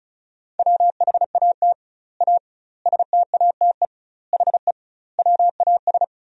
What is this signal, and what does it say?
Morse code, heard as a keyed tone, about 36 wpm, 700 Hz. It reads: WHAT A STATE HE WAS
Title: 2x Word-Spacing - Sentences from Top 400 Words - 35wpm